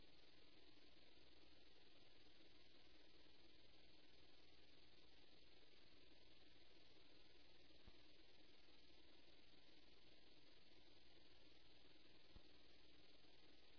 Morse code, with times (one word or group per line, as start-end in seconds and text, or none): none